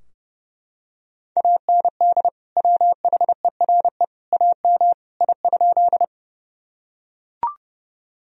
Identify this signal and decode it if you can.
Morse code, heard as a keyed tone, about 30 wpm, 700 Hz. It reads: AND WHERE AM I?
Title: Sentences from Top 1000 Words - Review Part 2 - 30wpm